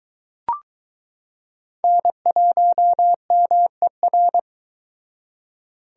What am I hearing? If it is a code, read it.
Morse code, heard as a keyed tone, about 22 wpm, 700 Hz. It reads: N1MER